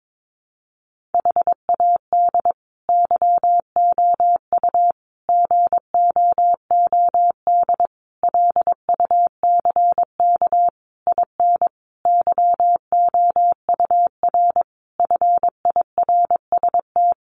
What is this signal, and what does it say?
Morse code, heard as a keyed tone, about 22 wpm, 700 Hz. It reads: HAD YOU GOOD LUCK IN YOUR FIRST
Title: Sentences from Top 1500 Words 22wpm